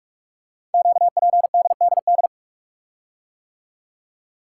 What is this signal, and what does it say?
Morse code, heard as a keyed tone, about 44 wpm, 700 Hz. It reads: QPDDD
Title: Code Groups - 5 Letters - 45wpm